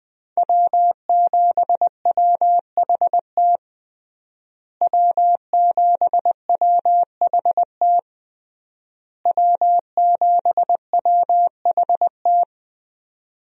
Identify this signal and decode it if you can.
Morse code, heard as a keyed tone, about 20 wpm, 700 Hz. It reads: W7WHT W7WHT W7WHT
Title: USA Call Signs - 5 Characters - Repeated 3x - 20wpm